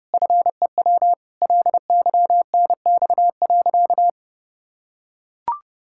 Morse code, heard as a keyed tone, about 30 wpm, 700 Hz. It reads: FEW LYNX.
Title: Pangram Challenge v2 - 30wpm